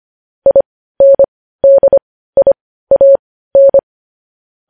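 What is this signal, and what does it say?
Morse code, heard as a keyed tone, about 25 wpm, spaced as for 15 wpm, 550 Hz. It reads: INDIAN
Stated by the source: Morse code news headlines at 15 WPM.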